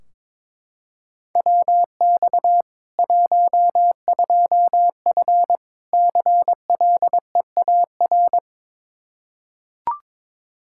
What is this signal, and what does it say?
Morse code, heard as a keyed tone, about 21 wpm, 700 Hz. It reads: WX 12F CLEAR
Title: Practice QSO - Exchange 2 - Info - 22wpm